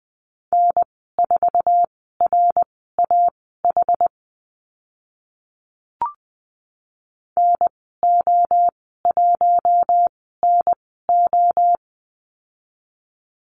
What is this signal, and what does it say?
Morse code, heard as a keyed tone, about 20 wpm, spaced as for 14 wpm, 700 Hz. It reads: N4RAH NO1NO
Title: 062 Lesson 11g - Letter h - Call Signs 20wpm